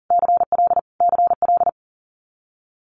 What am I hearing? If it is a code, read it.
Morse code, heard as a keyed tone, about 41 wpm, 700 Hz. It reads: CL CL
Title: Multi-letter Phonemes Repeated 3x 40wpm